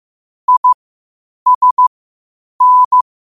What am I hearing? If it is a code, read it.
Morse code, heard as a keyed tone, about 15 wpm, spaced as for 8 wpm, 1000 Hz. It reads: ISN